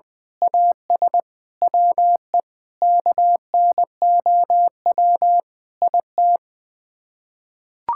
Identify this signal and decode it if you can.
Morse code, heard as a keyed tone, about 21 wpm, 700 Hz. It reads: AS WE KNOW IT
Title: Sets of 9 Words - 20wpm